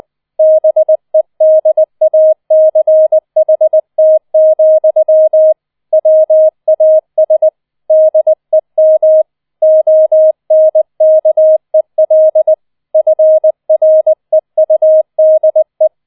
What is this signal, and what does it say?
Morse code, heard as a keyed tone, about 20 wpm, 630 Hz. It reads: BEDACHT, WAS DEM ONKEL FREUDE
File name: 20wpm